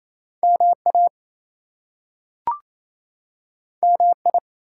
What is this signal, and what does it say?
Morse code, heard as a keyed tone, about 28 wpm, 700 Hz. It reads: MA MI
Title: US States and Canadian Provinces 28wpm